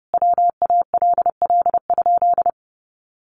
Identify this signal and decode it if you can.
Morse code, heard as a keyed tone, about 30 wpm, 700 Hz. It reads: WALL?
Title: Sentences from Top 1100 Words 30wpm